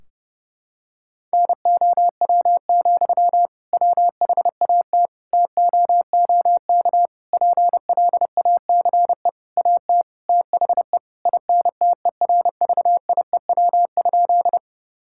Morse code, heard as a keyed tone, about 32 wpm, 700 Hz. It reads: NOW, WHAT TOOK PLACE AT THE INTERVIEW?
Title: Sentences from Top 1600 Words 30wpm